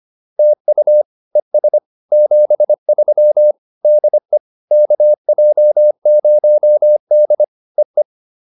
Morse code, heard as a keyed tone, about 25 wpm, 600 Hz. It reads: TU ES 73 DE KJ0D EE